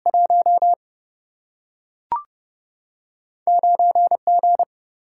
Morse code, heard as a keyed tone, about 30 wpm, 700 Hz. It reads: 1 9G
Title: Single-Letter Single-Number 30wpm